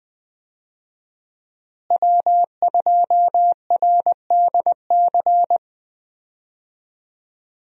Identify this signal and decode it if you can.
Morse code, heard as a keyed tone, about 20 wpm, 700 Hz. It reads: W2RDC
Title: USA Call Signs - 5 Characters - Repeated 3x - 20wpm